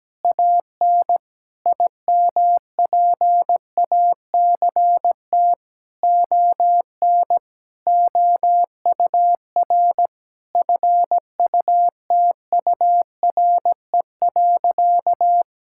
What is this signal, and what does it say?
Morse code, heard as a keyed tone, about 17 wpm, 700 Hz. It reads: AN IMPACT ON OUR FUTURE.
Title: Sentences from Top 1000 Words - Review Part 6 - 17wpm